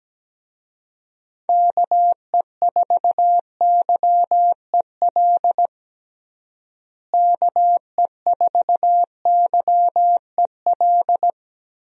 Morse code, heard as a keyed tone, about 17 wpm, 700 Hz. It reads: KE4YEL KE4YEL
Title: USA Call Signs - 6 Characters - Repeated 3x - 17wpm